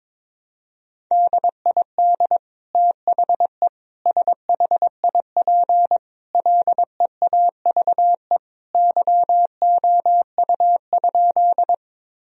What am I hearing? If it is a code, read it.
Morse code, heard as a keyed tone, about 21 wpm, 700 Hz. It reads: DID THE SHIP LEAVE YOU?
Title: Sentences from Top 1000 Words - Review Part 2 - 22wpm